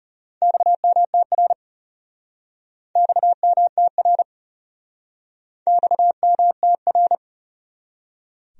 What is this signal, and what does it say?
Morse code, heard as a keyed tone, about 35 wpm, 700 Hz. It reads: XMTR XMTR XMTR
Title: Speed-Racing Abbreviations - 30wpm